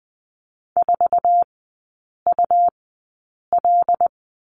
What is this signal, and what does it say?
Morse code, heard as a keyed tone, about 20 wpm, spaced as for 8 wpm, 700 Hz. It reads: 4UL
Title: 100 Lesson 17e - Letter u - Three Characters 20wpm